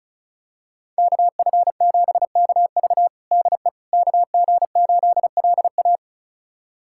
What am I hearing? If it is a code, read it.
Morse code, heard as a keyed tone, about 36 wpm, 700 Hz. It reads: KF7KV DE KG8LA
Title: USA Callsign DE Callsign - 5 Characters - 35wpm